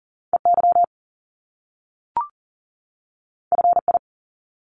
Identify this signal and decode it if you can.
Morse code, heard as a keyed tone, about 38 wpm, 700 Hz. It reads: EY FI
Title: Top Two Letter Combinations in Words 40wpm